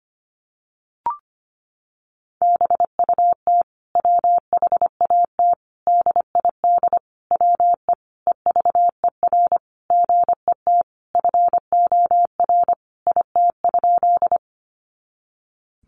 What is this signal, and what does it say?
Morse code, heard as a keyed tone, about 25 wpm, 700 Hz. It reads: BUT WHAT DID WE EVER GET FOR IT?
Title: Sentences from Top 400 Words 25wpm